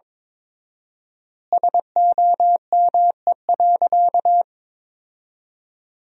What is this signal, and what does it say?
Morse code, heard as a keyed tone, about 22 wpm, 700 Hz. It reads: SOME.
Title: 4x Word-Spacing - Sentences from Top 100 Words - 22wpm